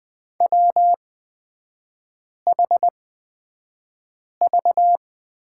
Morse code, heard as a keyed tone, about 20 wpm, spaced as for 5 wpm, 700 Hz. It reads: WHV
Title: 224 Lesson 36e - BT - Three Characters 20wpm